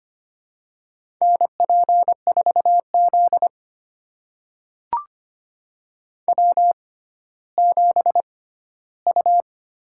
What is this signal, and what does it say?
Morse code, heard as a keyed tone, about 25 wpm, spaced as for 14 wpm, 700 Hz. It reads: NP4Z W7U
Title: USA Call Signs - 4 Characters - 6x Character Spacing - 25wpm